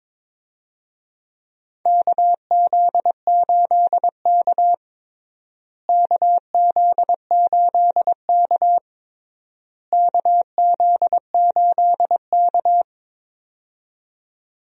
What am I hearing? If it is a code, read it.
Morse code, heard as a keyed tone, about 22 wpm, 700 Hz. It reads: KZ8K KZ8K KZ8K
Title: USA Call Signs - 4 Characters - Repeated 3x - 22wpm